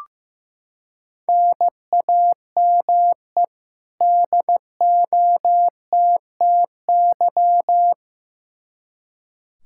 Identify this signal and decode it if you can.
Morse code, heard as a keyed tone, about 15 wpm, 700 Hz. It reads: NAME DOTTY
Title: K1USN SST QSO Practice 15wpm